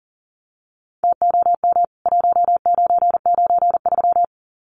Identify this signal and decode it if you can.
Morse code, heard as a keyed tone, about 40 wpm, 700 Hz. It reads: TOM 1993
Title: CWT Exchange 40wpm